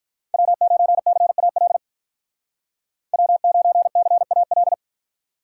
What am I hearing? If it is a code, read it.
Morse code, heard as a keyed tone, about 46 wpm, 700 Hz. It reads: W9CAL W9CAL